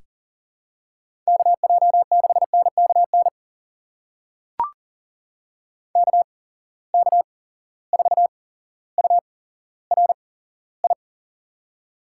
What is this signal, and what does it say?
Morse code, heard as a keyed tone, about 40 wpm, spaced as for 17 wpm, 700 Hz. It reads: KJ6NKN KK4URI
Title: USA Call Signs - 6 Characters - 8x Character Spacing - 40wpm